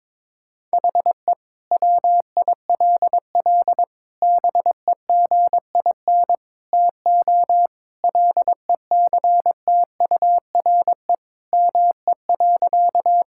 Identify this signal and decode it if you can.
Morse code, heard as a keyed tone, about 22 wpm, 700 Hz. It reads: HE WILL BEGIN TO LECTURE ME.
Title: Sentences from Top 1600 Words 22wpm